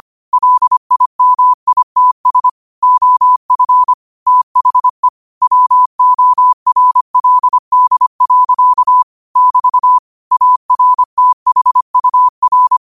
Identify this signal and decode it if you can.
Morse code, heard as a keyed tone, about 25 wpm, 1000 Hz. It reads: LIMITS OF THE WORLD. <BT> ARTHUR
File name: qotdfiles / 25wpm